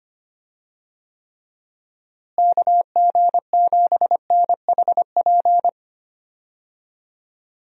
Morse code, heard as a keyed tone, about 25 wpm, 700 Hz. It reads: KG7NHP
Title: USA Call Signs - 6 Characters - Repeated 3x - 25wpm